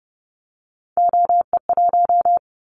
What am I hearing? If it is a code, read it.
Morse code, heard as a keyed tone, about 30 wpm, 700 Hz. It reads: OE1
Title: Callsign Prefixes for Top 100 DXCC - 30wpm